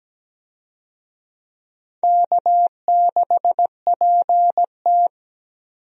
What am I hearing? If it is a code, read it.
Morse code, heard as a keyed tone, about 17 wpm, 700 Hz. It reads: K6PT
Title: USA Call Signs - 4 Characters - 17wpm